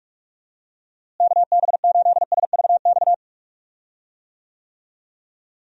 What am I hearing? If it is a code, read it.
Morse code, heard as a keyed tone, about 42 wpm, 700 Hz. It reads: KB8SVX
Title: USA Call Signs - 6 Characters - Repeated 2x - 45wpm